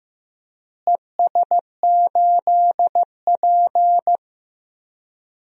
Morse code, heard as a keyed tone, about 15 wpm, 700 Hz. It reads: ES8P
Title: DXCC-100 Call Signs - 4 Characters - Repeated 2x - 15wpm